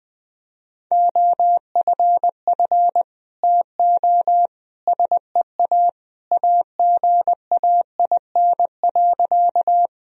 Morse code, heard as a keyed tone, about 19 wpm, 700 Hz. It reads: OFF TO SEA AGAIN.
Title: Sentences from Top 1000 Words - Review Part 1 - 20wpm